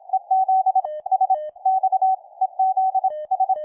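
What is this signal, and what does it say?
Morse code, heard as a keyed tone, about 26 wpm, 760 Hz. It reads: EZ S X EZ S
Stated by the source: Full QSK operation with a PFR-3A QRP transceiver, tested on a weak station's signal, Velko, interrupting his CQ.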